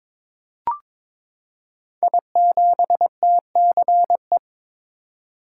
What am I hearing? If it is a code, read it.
Morse code, heard as a keyed tone, about 23 wpm, 700 Hz. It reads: I7TCE
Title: Code Groups - 5 Letter:Numbers - 22wpm